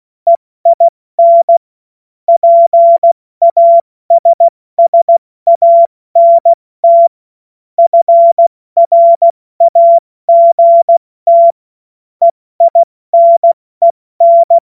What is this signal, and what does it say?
Morse code, heard as a keyed tone, about 16 wpm, 680 Hz. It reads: EIN PASSANT FRAGT EINEN
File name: Morsekurs / Witze / Witz 20 Februar 2024 70 F80